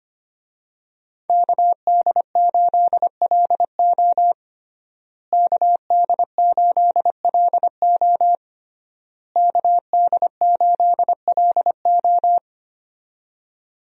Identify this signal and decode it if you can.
Morse code, heard as a keyed tone, about 25 wpm, 700 Hz. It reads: KD8LO KD8LO KD8LO
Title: USA Call Signs - 5 Characters - Repeated 3x - 25wpm